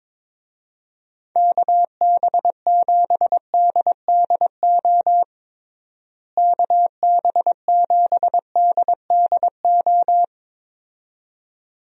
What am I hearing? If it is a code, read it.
Morse code, heard as a keyed tone, about 22 wpm, 700 Hz. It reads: KB7DDO KB7DDO
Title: USA Call Signs - 6 Characters - Repeated 2x - 22wpm